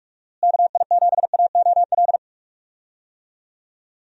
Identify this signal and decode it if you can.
Morse code, heard as a keyed tone, about 44 wpm, 700 Hz. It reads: KI7AOL
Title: USA Call Signs - 6 Characters - Repeated 3x - 45wpm